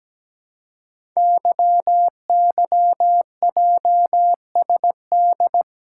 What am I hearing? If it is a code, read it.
Morse code, heard as a keyed tone, about 17 wpm, 700 Hz. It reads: YYJSD
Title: Code Groups - 5 Letters - 17wpm